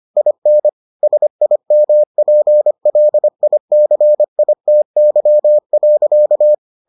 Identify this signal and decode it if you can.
Morse code, heard as a keyed tone, about 25 wpm, 600 Hz. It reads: IN SIMPLICITY.